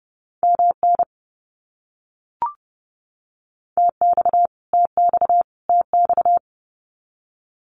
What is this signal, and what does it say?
Morse code, heard as a keyed tone, about 30 wpm, 700 Hz. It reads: MN TX TX TX
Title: US States Repeated 3x 30wpm